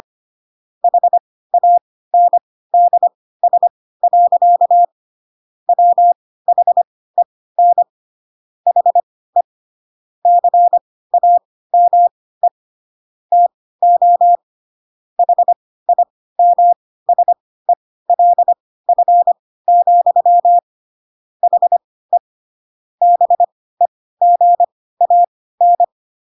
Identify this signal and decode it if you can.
Morse code, heard as a keyed tone, about 25 wpm, spaced as for 16 wpm, 700 Hz. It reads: HANDS. WHEN HE CAME TO HIMSELF, HE BEGAN